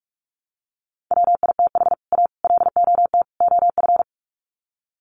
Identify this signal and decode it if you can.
Morse code, heard as a keyed tone, about 44 wpm, 700 Hz. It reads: WITH A LOT OF
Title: Sets of 4 Words 45wpm